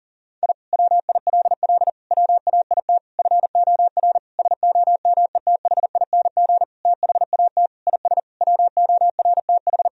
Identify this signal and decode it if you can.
Morse code, heard as a keyed tone, about 40 wpm, 700 Hz. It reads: I WILL WAIT FOR SOMETHING THAT IS WORTH
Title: Sentences from Top 1000 Words - Review Part 1 - 40wpm